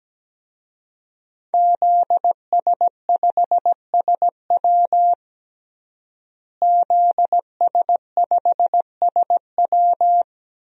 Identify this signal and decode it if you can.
Morse code, heard as a keyed tone, about 17 wpm, 700 Hz. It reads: ZS5SW ZS5SW
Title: DXCC-100 Call Signs - 5 Characters - Repeated 2x - 17wpm